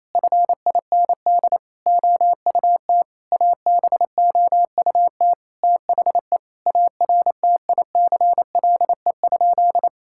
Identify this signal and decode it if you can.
Morse code, heard as a keyed tone, about 28 wpm, 700 Hz. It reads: FIND OUT ABOUT THE ARTICLE?
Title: Sentences from Top 1000 Words - Review Part 5 - 28wpm